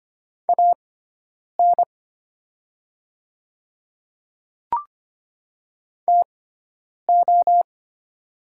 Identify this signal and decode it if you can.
Morse code, heard as a keyed tone, about 25 wpm, spaced as for 9 wpm, 700 Hz. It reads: AN TO
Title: 6x Character Spacing - Top 100 Words - 25wpm